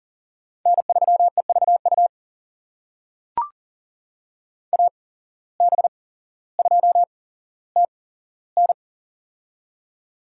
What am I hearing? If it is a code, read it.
Morse code, heard as a keyed tone, about 40 wpm, spaced as for 17 wpm, 700 Hz. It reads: N3EVU AB2TN
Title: USA Call Signs - 5 Characters - 8x Character Spacing - 40wpm